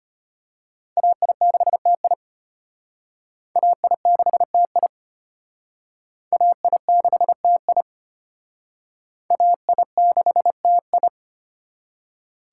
Morse code, heard as a keyed tone, about 32 wpm, 700 Hz. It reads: AI6TI AI6TI AI6TI AI6TI